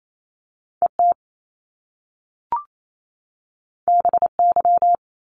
Morse code, heard as a keyed tone, about 28 wpm, 700 Hz. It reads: ET BY
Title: Top Two Letter Combinations in Words 28wpm